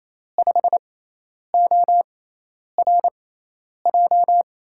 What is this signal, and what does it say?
Morse code, heard as a keyed tone, about 28 wpm, spaced as for 10 wpm, 700 Hz. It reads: 5ORJ
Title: USA Call Signs - 6 Characters - 6x Character Spacing - 28wpm